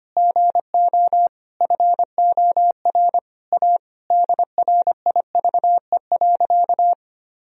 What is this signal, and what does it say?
Morse code, heard as a keyed tone, about 25 wpm, 700 Hz. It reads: GO FOR A DRIVE.
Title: Sentences from Top 1000 Words - Review Part 2 - 25wpm